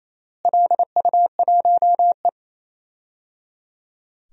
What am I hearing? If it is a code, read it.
Morse code, heard as a keyed tone, about 28 wpm, 700 Hz. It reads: LU1E
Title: DXCC-100 Call Signs - 4 Characters - Repeated 3x - 28wpm